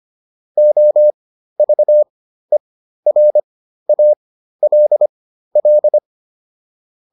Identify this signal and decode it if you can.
Morse code, heard as a keyed tone, about 25 wpm, spaced as for 13 wpm, 600 Hz. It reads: OVERALL